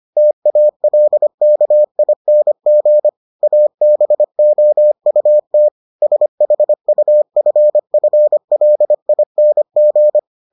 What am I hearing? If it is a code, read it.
Morse code, heard as a keyed tone, about 25 wpm, 600 Hz. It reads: TALKING ABOUT SHUFFLING